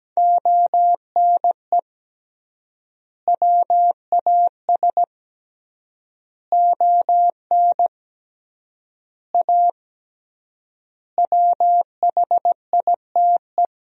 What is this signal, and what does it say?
Morse code, heard as a keyed tone, about 17 wpm, 700 Hz. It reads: ONE WAS ON A WHITE
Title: 3x Word-Spacing - Sentences from Top 300 Words - 17wpm